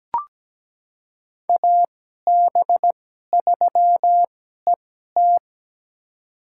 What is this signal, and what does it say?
Morse code, heard as a keyed tone, about 17 wpm, spaced as for 12 wpm, 700 Hz. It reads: AB3ET
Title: USA Call Signs - 5 Characters - 2x Character Spacing - 17wpm